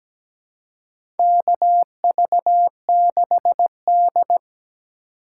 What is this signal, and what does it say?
Morse code, heard as a keyed tone, about 17 wpm, 700 Hz. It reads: KV6D
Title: USA Call Signs - 3x Character Spacing - 17wpm